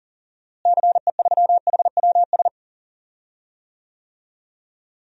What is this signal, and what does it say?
Morse code, heard as a keyed tone, about 40 wpm, 700 Hz. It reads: CE3HWS
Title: DXCC-100 Call Signs - 6 Characters - Repeated 3x - 40wpm